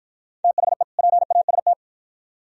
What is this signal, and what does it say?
Morse code, heard as a keyed tone, about 48 wpm, 700 Hz. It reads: THE PAST